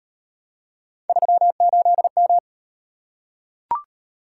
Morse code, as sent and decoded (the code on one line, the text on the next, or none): ...-- ---.. --
38M